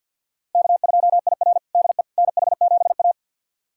Morse code, heard as a keyed tone, about 52 wpm, 700 Hz. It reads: K2IR DE NH7A
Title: USA Callsign DE Callsign - 4 Characters - 50wpm